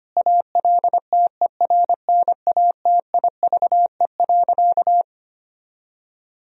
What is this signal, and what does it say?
Morse code, heard as a keyed tone, about 25 wpm, 700 Hz. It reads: ALTERNATIVE.
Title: Sentences from Top 1000 Words - Review Part 4 - 25wpm